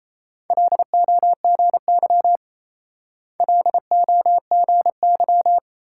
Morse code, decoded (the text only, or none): LOGY LOGY